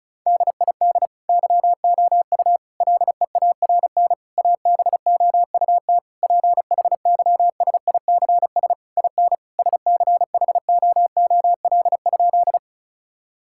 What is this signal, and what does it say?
Morse code, heard as a keyed tone, about 34 wpm, 700 Hz. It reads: DID YOU LEARN ABOUT PHYSICS IN SCHOOL?
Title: Sentences from Top 1000 Words - Review Part 5 - 35wpm